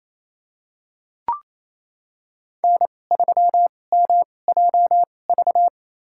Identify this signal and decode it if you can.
Morse code, heard as a keyed tone, about 27 wpm, spaced as for 20 wpm, 700 Hz. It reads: N3MJV
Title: USA Call Signs - 5 Characters - 2x Character Spacing - 28wpm